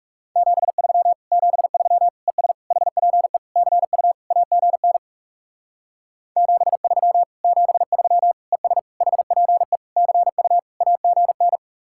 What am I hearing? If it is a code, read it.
Morse code, heard as a keyed tone, about 36 wpm, 700 Hz. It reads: 73 73 ES HPE CU AGN 73 73 ES HPE CU AGN
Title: Speed-Racing Common QSO Phrases - 30wpm